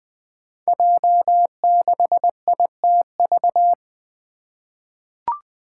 Morse code, heard as a keyed tone, about 20 wpm, 700 Hz. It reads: J6ITV
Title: Five Challenging Letter-Number Combinations 20wpm